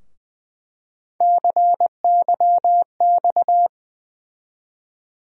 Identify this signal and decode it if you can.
Morse code, heard as a keyed tone, about 20 wpm, 700 Hz. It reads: CYX
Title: USA Call Sign Postfixes - 3 Letters 20wpm